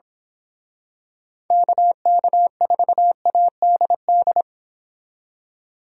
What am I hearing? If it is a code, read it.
Morse code, heard as a keyed tone, about 26 wpm, 700 Hz. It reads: KK4ADD
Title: Speed Racing - USA 2x3 Call Signs 22wpm